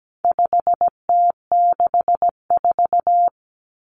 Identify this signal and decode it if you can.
Morse code, heard as a keyed tone, about 17 wpm, 700 Hz. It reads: 5T64
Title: Common Cut Numbers - 4-Digit 17wpm